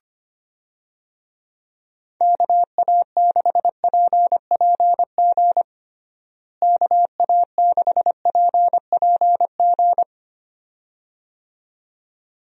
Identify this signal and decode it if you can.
Morse code, heard as a keyed tone, about 25 wpm, 700 Hz. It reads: KA6PPG KA6PPG
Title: USA Call Signs - 6 Characters - Repeated 2x - 25wpm